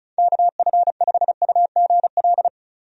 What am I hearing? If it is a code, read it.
Morse code, heard as a keyed tone, about 36 wpm, 700 Hz. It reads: KF5UGL
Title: USA Call Signs - 6 Characters - 35wpm